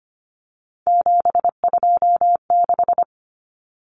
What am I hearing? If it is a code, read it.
Morse code, heard as a keyed tone, about 25 wpm, 700 Hz. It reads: 726
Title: Numbers - 3-Digit 25wpm